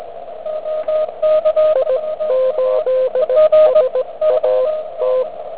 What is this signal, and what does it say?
Morse code, heard as a keyed tone, about 18 wpm, spaced as for 16 wpm, 510 Hz, with radio noise. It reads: I 8 SA T
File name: ARCHIV / zvuk3 / ZS1JX